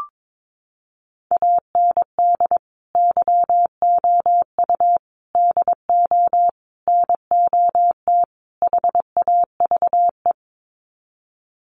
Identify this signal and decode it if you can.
Morse code, heard as a keyed tone, about 22 wpm, 700 Hz. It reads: AND YOU DO NOT HAVE
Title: Sets of 5 Words Encore 22wpm